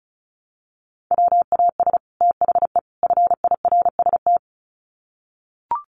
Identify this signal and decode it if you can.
Morse code, heard as a keyed tone, about 34 wpm, 700 Hz. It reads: WAS THE FIRST
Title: Sets of 3 Words 35wpm